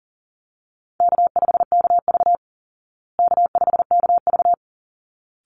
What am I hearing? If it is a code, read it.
Morse code, heard as a keyed tone, about 40 wpm, 700 Hz. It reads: K5KV K5KV
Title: Easy Call Signs Repeated 2x - 40wpm